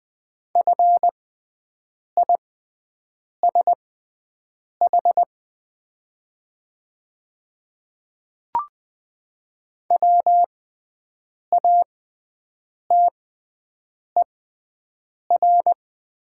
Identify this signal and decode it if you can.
Morse code, heard as a keyed tone, about 20 wpm, spaced as for 7 wpm, 700 Hz. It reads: FISH WATER
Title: 6x Character Spacing - Top 300 Words - 20wpm